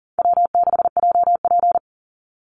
Morse code, heard as a keyed tone, about 40 wpm, 700 Hz. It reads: W6JP
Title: NA SOTA Chaser Callsigns 40wpm